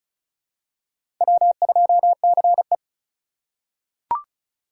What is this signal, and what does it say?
Morse code, heard as a keyed tone, about 36 wpm, 700 Hz. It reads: W2CE
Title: USA Call Signs - 4 Characters - 3x Character Spacing - 35wpm